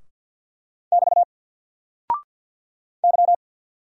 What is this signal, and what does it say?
Morse code, heard as a keyed tone, about 48 wpm, 700 Hz. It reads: <BT> Y
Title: Single Character (Letter:Number:Punctuation:Prosign) 50wpm